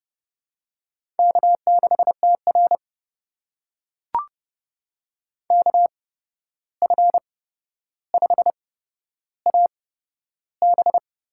Morse code, heard as a keyed tone, about 30 wpm, spaced as for 12 wpm, 700 Hz. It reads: K6TR KF5AB
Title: USA Call Signs - 8x Character Spacing - 30wpm